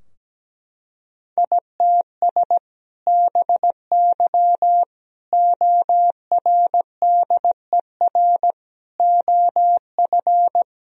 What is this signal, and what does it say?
Morse code, heard as a keyed tone, about 17 wpm, 700 Hz. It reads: ITS BY ORDER OF
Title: Sentences from Top 1000 Words - Review Part 3 - 17wpm